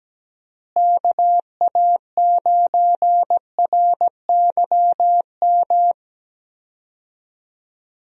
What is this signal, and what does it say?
Morse code, heard as a keyed tone, about 17 wpm, 700 Hz. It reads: KA9RYM
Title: USA Call Signs - 6 Characters - Repeated 3x - 17wpm